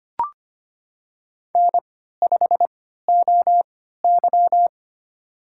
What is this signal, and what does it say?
Morse code, heard as a keyed tone, about 25 wpm, spaced as for 14 wpm, 700 Hz. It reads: N5OY
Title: USA Call Signs - 3x Character Spacing - 25wpm